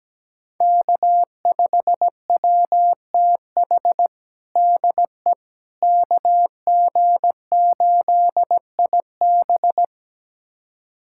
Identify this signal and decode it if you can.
Morse code, heard as a keyed tone, about 17 wpm, 700 Hz. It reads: K5WTH DE KG8IB
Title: USA Callsign DE Callsign - 5 Characters - 17wpm